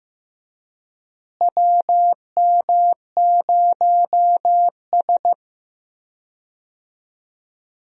Morse code, heard as a keyed tone, about 15 wpm, 700 Hz. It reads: WM0S
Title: USA Call Signs - 4 Characters - Repeated 3x - 15wpm